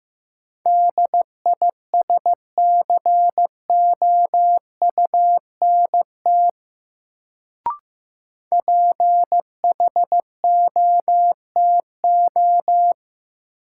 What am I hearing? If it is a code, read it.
Morse code, heard as a keyed tone, about 15 wpm, 700 Hz. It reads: DISCOUNT PHOTO
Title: Sentences from Top 1200 Words - New Word Drill - 15wpm